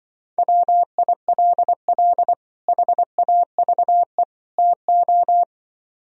Morse code, heard as a keyed tone, about 24 wpm, 700 Hz. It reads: WILL HAVE TO
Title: Speed-Racing - Sets of 3 Words 20wpm